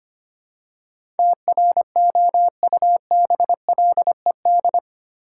Morse code, heard as a keyed tone, about 25 wpm, 700 Hz. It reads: TROUBLED
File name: 8 Letter Words - First 3 Letters Given - 25wpm